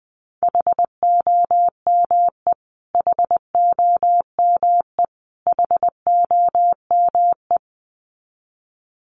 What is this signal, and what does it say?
Morse code, heard as a keyed tone, about 20 wpm, 700 Hz. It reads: HOME HOME HOME
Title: Top 200 Words Repeated 20wpm